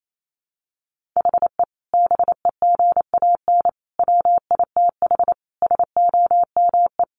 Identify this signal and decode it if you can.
Morse code, heard as a keyed tone, about 28 wpm, 700 Hz. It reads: HE BEGAN WITH SOME
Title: Sentences from Top 1500 Words 28wpm